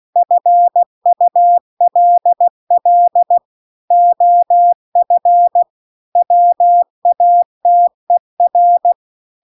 Morse code, heard as a keyed tone, about 16 wpm, 700 Hz. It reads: FULL OF WATER